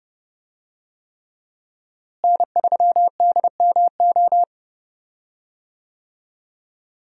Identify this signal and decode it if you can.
Morse code, heard as a keyed tone, about 30 wpm, 700 Hz. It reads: N3DMO
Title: USA Call Signs - 5 Characters - Repeated 3x - 30wpm